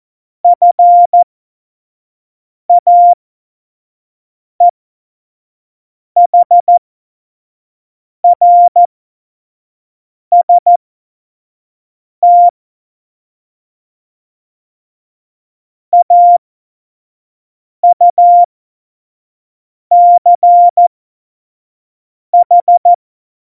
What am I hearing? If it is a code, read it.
Morse code, heard as a keyed tone, about 14 wpm, spaced as for 5 wpm, 700 Hz. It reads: FAEHRST AUCH
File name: Morsekurs / Witze / Witz 02 Juni 2020, 25 F70